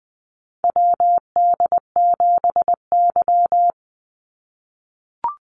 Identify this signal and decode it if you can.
Morse code, heard as a keyed tone, about 20 wpm, 700 Hz. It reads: WD7Y
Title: NA SOTA Chaser Callsigns 20wpm